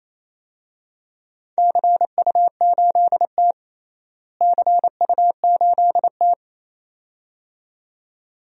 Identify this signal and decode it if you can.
Morse code, heard as a keyed tone, about 28 wpm, 700 Hz. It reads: CU8T CU8T
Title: DXCC-100 Call Signs - 4 Characters - Repeated 2x - 28wpm